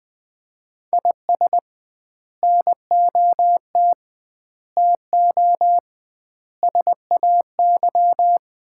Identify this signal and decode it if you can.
Morse code, heard as a keyed tone, about 21 wpm, 700 Hz. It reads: IS NOT TO SAY
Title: 2x Word-Spacing - Sets of 5 Words - 20wpm